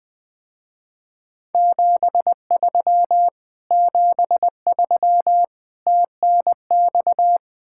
Morse code, heard as a keyed tone, about 20 wpm, 700 Hz. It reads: 73 73 TNX
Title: Speed-Racing Common QSO Phrases - 17wpm